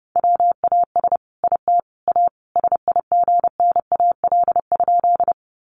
Morse code, heard as a keyed tone, about 30 wpm, 700 Hz. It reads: WAS IT A SIGNAL?
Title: Sentences from Top 1400 Words 30wpm